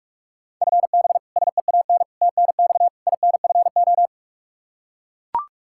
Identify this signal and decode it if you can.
Morse code, heard as a keyed tone, about 44 wpm, 700 Hz. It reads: FB SEAN TNX INFO
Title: Practice QSO - Exchange 3 - More Info - 45wpm